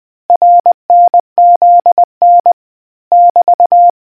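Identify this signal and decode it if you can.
Morse code, heard as a keyed tone, about 20 wpm, 700 Hz. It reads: RNZN <BT>
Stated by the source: Morse quiz question from a New Zealand navy communicators association.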